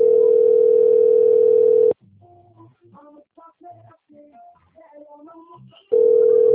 Ringback tone (North America), 440 + 480 Hz.
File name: out- - - - - .42917